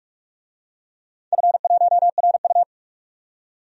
Morse code, heard as a keyed tone, about 46 wpm, 700 Hz. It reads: F1RU
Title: DXCC-100 Call Signs - 4 Characters - 45wpm